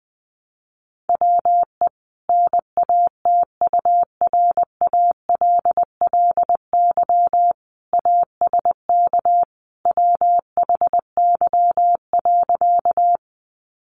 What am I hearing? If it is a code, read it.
Morse code, heard as a keyed tone, about 20 wpm, 700 Hz. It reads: WE NATURALLY ASK WHY.
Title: Sentences from Top 1300 Words 20wpm